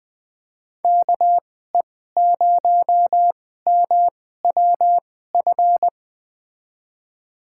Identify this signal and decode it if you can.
Morse code, heard as a keyed tone, about 21 wpm, spaced as for 15 wpm, 700 Hz. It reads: KE0MWF
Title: USA Call Signs - 2x Character Spacing - 20wpm